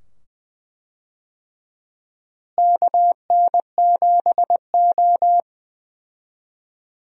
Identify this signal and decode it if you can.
Morse code, heard as a keyed tone, about 20 wpm, 700 Hz. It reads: KN7O
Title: USA Call Signs - 4 Characters - Repeated 2x - 20wpm